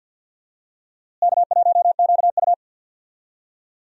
Morse code, heard as a keyed tone, about 51 wpm, 700 Hz. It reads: K1QU
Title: USA Call Signs - 4 Characters - Repeated 3x - 50wpm